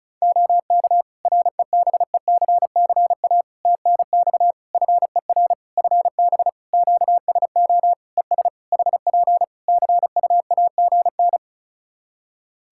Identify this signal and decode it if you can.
Morse code, heard as a keyed tone, about 34 wpm, 700 Hz. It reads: OK REBECCA TNX FER FB QSO ES HP CUAGN
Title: Practice QSO - Exchange 4 - Ending - 35wpm